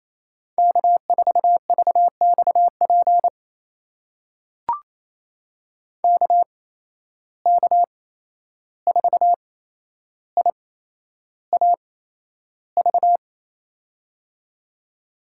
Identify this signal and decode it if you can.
Morse code, heard as a keyed tone, about 28 wpm, spaced as for 11 wpm, 700 Hz. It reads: K4VXP KK4IAV
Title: USA Call Signs - 8x Character Spacing - 28wpm